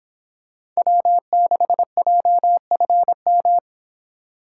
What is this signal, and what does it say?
Morse code, heard as a keyed tone, about 25 wpm, 700 Hz. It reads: W6JFM